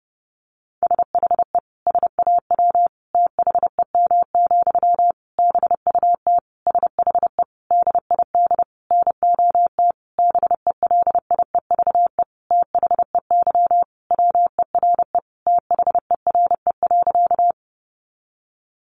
Morse code, heard as a keyed tone, about 30 wpm, 700 Hz. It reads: SHE SAW THEM, BUT SHE DID NOT BELIEVE THEY WERE THERE.
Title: Sentences from Top 1000 Words 30wpm